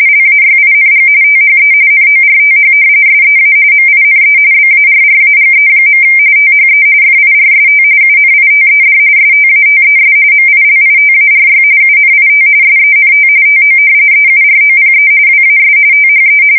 RTTY, about 45.45 baud. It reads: LAYING THE NORTH SEA MINE BARRAGE. THE FIRST UNITED STATES MILITARY UNIT SENT TO EUROPE WAS THE
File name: fsk170 / usnavy4